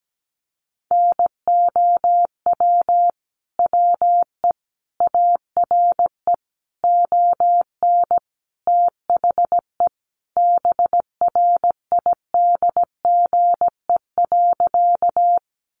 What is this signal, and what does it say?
Morse code, heard as a keyed tone, about 17 wpm, 700 Hz. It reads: NOW WE ARE ON THE BRIDGE.
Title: Sentences from Top 1200 Words 17wpm